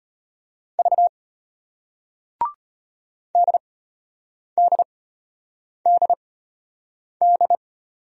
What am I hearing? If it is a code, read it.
Morse code, heard as a keyed tone, about 35 wpm, spaced as for 8 wpm, 700 Hz. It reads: V DDDD